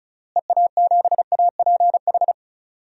Morse code, heard as a keyed tone, about 34 wpm, 700 Hz. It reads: EA7APH
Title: DXCC-100 Call Signs - 6 Characters - Repeated 3x - 35wpm